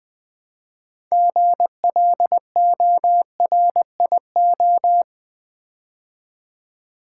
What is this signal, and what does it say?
Morse code, heard as a keyed tone, about 20 wpm, 700 Hz. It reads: GLORIO
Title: Build A Word - 3 Letters At A Time - 8-Letter Words - 20wpm